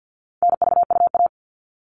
Morse code, heard as a keyed tone, about 50 wpm, 700 Hz. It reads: NVUA
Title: All Cut Numbers - 4-Digit 50wpm